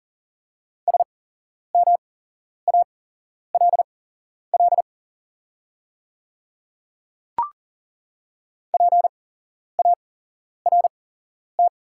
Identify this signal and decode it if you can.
Morse code, heard as a keyed tone, about 40 wpm, spaced as for 11 wpm, 700 Hz. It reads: SMALL PART
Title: 8x Character Spacing - Top 500 Words - 40wpm